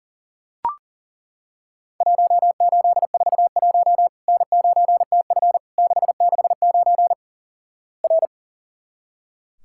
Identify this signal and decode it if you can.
Morse code, heard as a keyed tone, about 37 wpm, 700 Hz. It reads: 1841 N9TF 669 R
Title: Worked All Europe DX Contest - QTC Exchange - 40wpm